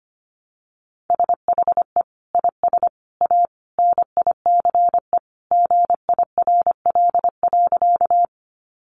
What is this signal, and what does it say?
Morse code, heard as a keyed tone, about 25 wpm, 700 Hz. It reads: SHE IS A NICE GIRL.
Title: Sentences from Top 1100 Words 25wpm